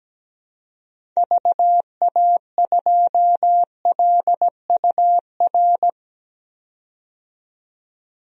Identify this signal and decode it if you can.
Morse code, heard as a keyed tone, about 17 wpm, 700 Hz. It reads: VA2LUR
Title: DXCC-100 Call Signs - 6 Characters - 17wpm